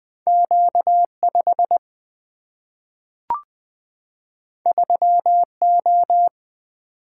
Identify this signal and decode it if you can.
Morse code, heard as a keyed tone, about 20 wpm, 700 Hz. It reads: Q 5 3 O
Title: Single-Letter Single-Number 20wpm